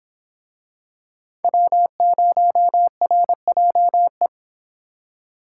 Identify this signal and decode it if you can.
Morse code, heard as a keyed tone, about 25 wpm, 700 Hz. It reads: W0RJE